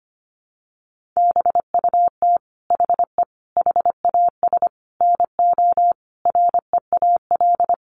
Morse code, heard as a keyed tone, about 25 wpm, 700 Hz. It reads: BUT HE HAS NO REAL
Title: Sentences from Top 1300 Words 25wpm